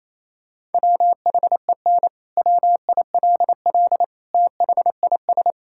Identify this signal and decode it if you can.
Morse code, heard as a keyed tone, about 27 wpm, 700 Hz. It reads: WHEN WILL THIS
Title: Sentences from Top 1600 Words 28wpm